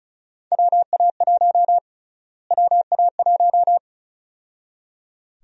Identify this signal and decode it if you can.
Morse code, heard as a keyed tone, about 34 wpm, 700 Hz. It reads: WA1 WA1
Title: USA Call Sign Prefixes - Repeated 2x - 35wpm